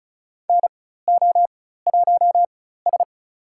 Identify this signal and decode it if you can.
Morse code, heard as a keyed tone, about 34 wpm, spaced as for 16 wpm, 700 Hz. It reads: NO1S
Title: USA Call Signs - 4 Characters - 4x Character Spacing - 35wpm